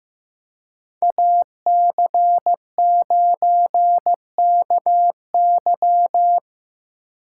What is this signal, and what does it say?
Morse code, heard as a keyed tone, about 18 wpm, 700 Hz. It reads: AC9KY
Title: Speed Racing - USA 2x2 Call Signs 15wpm